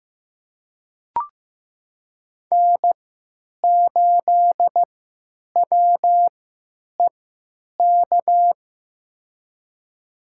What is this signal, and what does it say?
Morse code, heard as a keyed tone, about 15 wpm, spaced as for 9 wpm, 700 Hz. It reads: N8WEK
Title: USA Call Signs - 3x Character Spacing - 15wpm